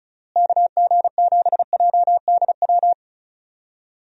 Morse code, heard as a keyed tone, about 34 wpm, 700 Hz. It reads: KG7JDW
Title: USA Call Signs - 6 Characters - Repeated 2x - 35wpm